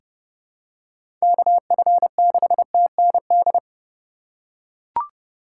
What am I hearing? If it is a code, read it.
Morse code, heard as a keyed tone, about 29 wpm, 700 Hz. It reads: KF6TND
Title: USA Call Signs - 6 Characters - 4x Character Spacing - 30wpm